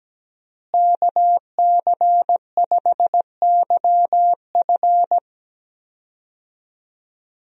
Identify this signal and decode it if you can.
Morse code, heard as a keyed tone, about 17 wpm, 700 Hz. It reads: KC5YF
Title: USA Call Signs - 5 Characters - Repeated 2x - 17wpm